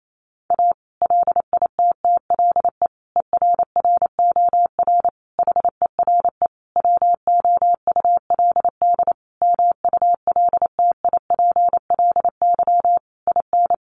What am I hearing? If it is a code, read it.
Morse code, heard as a keyed tone, about 28 wpm, 700 Hz. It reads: A LITTLE ERROR HERE WOULD MULTIPLY IN
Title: A Sound Of Thunder 28wpm